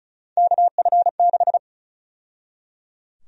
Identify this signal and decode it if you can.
Morse code, heard as a keyed tone, about 36 wpm, 700 Hz. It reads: KF6
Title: USA Call Sign Prefixes - 35wpm